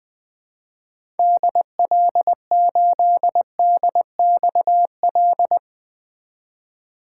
Morse code, heard as a keyed tone, about 20 wpm, 700 Hz. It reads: DL8DXL
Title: 2023 SOTA Chaser Call Signs - 20wpm